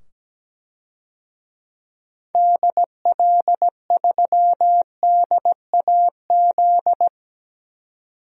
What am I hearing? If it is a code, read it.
Morse code, heard as a keyed tone, about 17 wpm, 700 Hz. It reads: DL3DAZ
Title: DXCC-100 Call Signs - 6 Characters - 17wpm